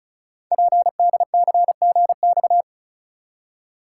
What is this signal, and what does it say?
Morse code, heard as a keyed tone, about 36 wpm, 700 Hz. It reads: PDCGX
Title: Code Groups - 5 Letters - 35wpm